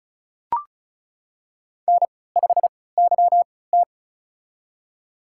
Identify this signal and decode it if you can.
Morse code, heard as a keyed tone, about 36 wpm, spaced as for 20 wpm, 700 Hz. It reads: N5YT
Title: USA Call Signs - 3x Character Spacing - 35wpm